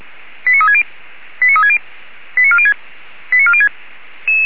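Five-tone selective call (ZVEI), heard as quick runs of five tones, tones of about 70 ms. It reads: 89360 89360 89406 89406
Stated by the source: ZVEI five-tone selective call received on an SDR.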